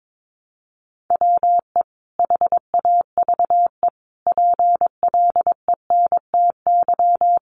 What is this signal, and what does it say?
Morse code, heard as a keyed tone, about 22 wpm, 700 Hz. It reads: WE HAVE PLENTY
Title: Sentences from Top 1300 Words 22wpm